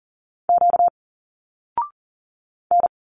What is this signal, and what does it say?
Morse code, heard as a keyed tone, about 40 wpm, 700 Hz. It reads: Q N
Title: Single Letters 40wpm